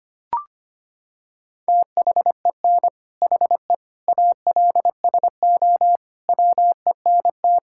Morse code, heard as a keyed tone, about 25 wpm, 700 Hz. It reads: THEN HE ALSO WENT
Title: Sentences from Top 1600 Words 25wpm